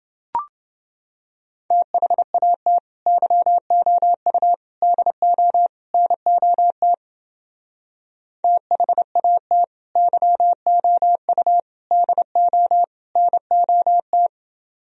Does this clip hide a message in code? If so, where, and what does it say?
Yes: Morse code from 1.7 to 14.3 s, heard as a keyed tone, about 28 wpm, 700 Hz. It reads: THAT YOU DO NOT THAT YOU DO NOT